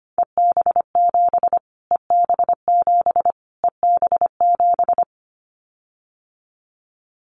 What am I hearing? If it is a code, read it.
Morse code, heard as a keyed tone, about 25 wpm, 700 Hz. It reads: EB7 EB7 EB7
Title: Callsign Prefixes for Top 25 DXCC - Repeated 3x - 25wpm